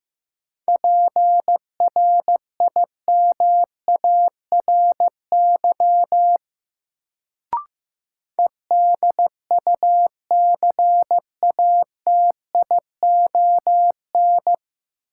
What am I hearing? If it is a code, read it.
Morse code, heard as a keyed tone, about 15 wpm, 700 Hz. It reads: PRIMARY EDUCATION
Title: Sentences from Top 1100 Words - New Word Drill Alt - 15wpm